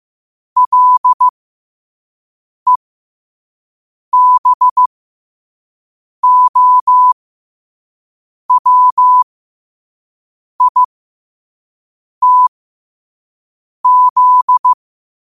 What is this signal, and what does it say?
Morse code, heard as a keyed tone, about 15 wpm, spaced as for 5 wpm, 1000 Hz. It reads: LEBOWITZ
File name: qotdfiles / 5wpm